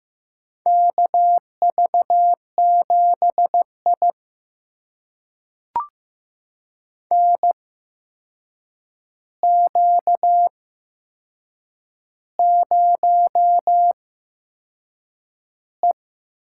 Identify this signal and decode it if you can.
Morse code, heard as a keyed tone, about 15 wpm, spaced as for 6 wpm, 700 Hz. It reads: KV7I NQ0E
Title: USA Call Signs - 4 Characters - 8x Character Spacing - 15wpm